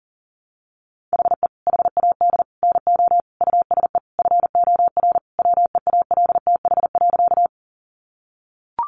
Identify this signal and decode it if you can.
Morse code, heard as a keyed tone, about 40 wpm, 700 Hz. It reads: HE HAD NO USE FOR WEALTH.
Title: Sentences from Top 1200 Words 40wpm